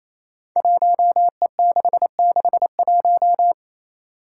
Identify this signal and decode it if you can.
Morse code, heard as a keyed tone, about 28 wpm, 700 Hz. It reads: 1E661
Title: Five Challenging Letter-Number Combinations 28wpm